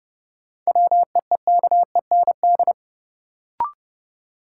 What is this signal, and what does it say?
Morse code, heard as a keyed tone, about 30 wpm, 700 Hz. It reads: WEEKEND
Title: Sets of 5 Words - New Word Drill - 30wpm